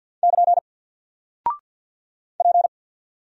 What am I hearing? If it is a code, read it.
Morse code, heard as a keyed tone, about 48 wpm, 700 Hz. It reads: <KN> P
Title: Single Character (Letter:Number:Punctuation:Prosign) Alt 50wpm